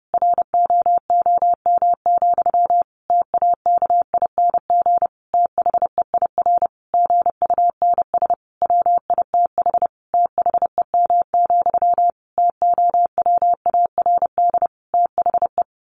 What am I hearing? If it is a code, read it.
Morse code, heard as a keyed tone, about 30 wpm, 700 Hz. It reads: ROOM, TAKING THEIR GUNS WITH THEM, TOWARD THE
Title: A Sound Of Thunder 30wpm